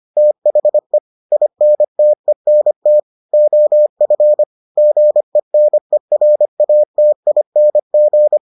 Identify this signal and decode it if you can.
Morse code, heard as a keyed tone, about 25 wpm, 600 Hz. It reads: THE INTENT OF GENERATING